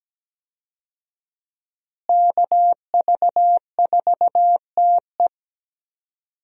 Morse code, heard as a keyed tone, about 17 wpm, 700 Hz. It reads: KV4TE
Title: USA Call Signs - 5 Characters - Repeated 2x - 17wpm